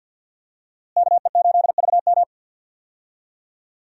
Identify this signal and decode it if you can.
Morse code, heard as a keyed tone, about 52 wpm, 700 Hz. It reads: KE8VM
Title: USA Call Signs - 5 Characters - Repeated 3x - 50wpm